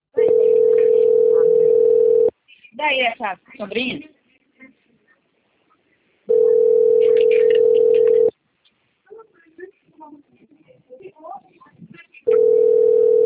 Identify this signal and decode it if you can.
Ringback tone (North America), 440 + 480 Hz, with speech.
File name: out- - - - - .95679